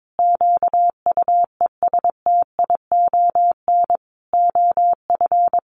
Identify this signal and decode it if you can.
Morse code, heard as a keyed tone, about 22 wpm, 700 Hz. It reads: QUESTION OF
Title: Sentences from Top 1400 Words 22wpm